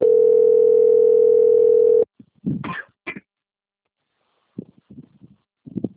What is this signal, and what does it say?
Ringback tone (North America), 440 + 480 Hz.
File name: out- - - - - .7259